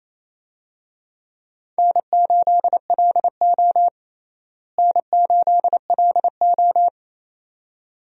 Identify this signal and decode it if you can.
Morse code, heard as a keyed tone, about 27 wpm, 700 Hz. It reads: N8LO N8LO
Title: USA Call Signs - 4 Characters - Repeated 2x - 28wpm